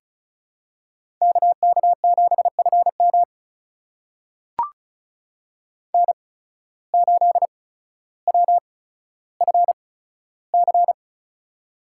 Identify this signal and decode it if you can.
Morse code, heard as a keyed tone, about 36 wpm, spaced as for 15 wpm, 700 Hz. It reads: KK7FM N8WFC
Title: USA Call Signs - 5 Characters - 8x Character Spacing - 35wpm